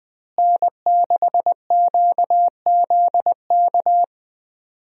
Morse code, heard as a keyed tone, about 20 wpm, 700 Hz. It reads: N6QZK
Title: USA Call Signs - 5 Characters - Repeated 3x - 20wpm